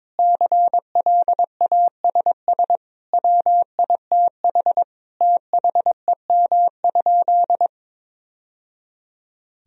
Morse code, heard as a keyed tone, about 21 wpm, 700 Hz. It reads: CLASS WITH THEM?
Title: Sentences from Top 1000 Words - Review Part 2 - 22wpm